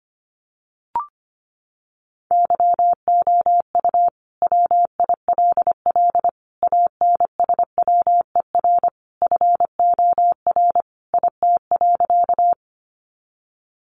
Morse code, heard as a keyed tone, about 25 wpm, 700 Hz. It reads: YOU WILL ANSWER FOR IT.
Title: Sentences from Top 300 Words 25wpm